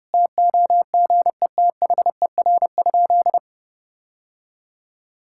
Morse code, heard as a keyed tone, about 30 wpm, 700 Hz. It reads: TOGETHER?
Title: Sentences from Top 1600 Words 30wpm